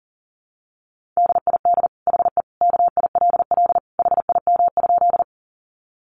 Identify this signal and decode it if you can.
Morse code, heard as a keyed tone, about 40 wpm, 700 Hz. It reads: DID HE KILL HIM?
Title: Sentences from Top 1400 Words 40wpm